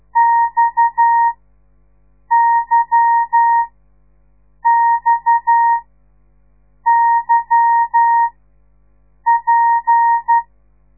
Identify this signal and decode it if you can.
Morse code, heard as a keyed tone, about 12 wpm, spaced as for 6 wpm, 930 Hz. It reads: XYXYP
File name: radiomorningstar / morse / les 8